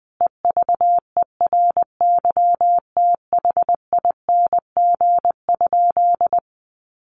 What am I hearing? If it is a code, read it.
Morse code, heard as a keyed tone, about 20 wpm, 700 Hz. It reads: EVERYTHING?
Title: Sentences from Top 700 Words 20wpm